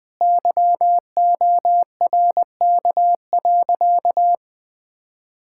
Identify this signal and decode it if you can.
Morse code, heard as a keyed tone, about 20 wpm, 700 Hz. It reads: YORK.
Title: Sentences with 3-Word Phrases - Spoken Beginning - 20wpm